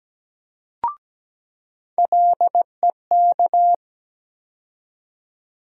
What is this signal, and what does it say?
Morse code, heard as a keyed tone, about 17 wpm, 700 Hz. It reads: LEK
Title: USA Call Sign Postfixes - 3 Letters 17wpm